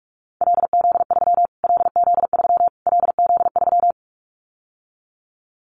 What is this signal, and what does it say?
Morse code, heard as a keyed tone, about 44 wpm, 700 Hz. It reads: LZ3 LZ3 LZ3
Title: Callsign Prefixes for Top 25 DXCC - Repeated 3x - 45wpm